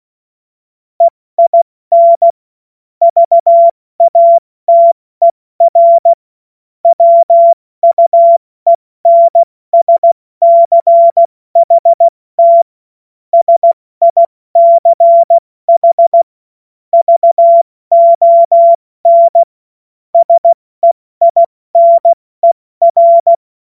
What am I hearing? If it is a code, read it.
Morse code, heard as a keyed tone, about 16 wpm, 680 Hz. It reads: EIN VATER WUENSCHT SICH VON SEINER
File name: Morsekurs / Witze / Witz 30. Mai 2023 70 F80